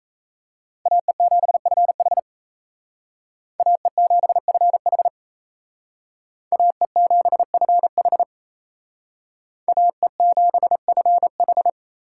Morse code, heard as a keyed tone, about 35 wpm, 700 Hz. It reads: AE7FH AE7FH AE7FH AE7FH